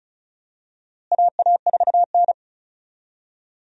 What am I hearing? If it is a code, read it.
Morse code, heard as a keyed tone, about 34 wpm, 700 Hz. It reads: AA4N
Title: USA Call Signs - 4 Characters - Repeated 3x - 35wpm